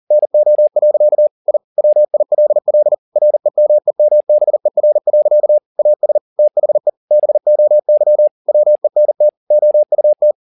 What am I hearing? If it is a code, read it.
Morse code, heard as a keyed tone, about 40 wpm, 600 Hz. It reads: NO. I WILL REMEMBER. AS THE BOY WENT OUT
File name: the old man and the sea 2 40wpm